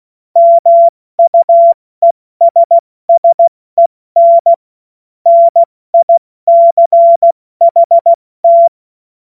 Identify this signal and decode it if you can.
Morse code, heard as a keyed tone, about 16 wpm, 680 Hz. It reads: MUESSEN NICHT
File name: Morsekurs / Witze / Witz 22. Juni 2021, 70 F80